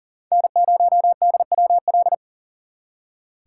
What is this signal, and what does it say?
Morse code, heard as a keyed tone, about 52 wpm, 700 Hz. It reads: N0DWL